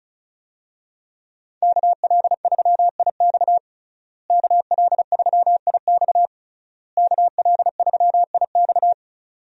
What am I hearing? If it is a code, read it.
Morse code, heard as a keyed tone, about 34 wpm, 700 Hz. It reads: KL3IX KL3IX KL3IX
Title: USA Call Signs - 5 Characters - Repeated 3x - 35wpm